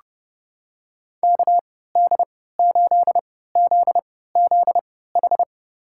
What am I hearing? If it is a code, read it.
Morse code, heard as a keyed tone, about 31 wpm, spaced as for 17 wpm, 700 Hz. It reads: KD8ZZH
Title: USA Call Signs - 6 Characters - 3x Character Spacing - 30wpm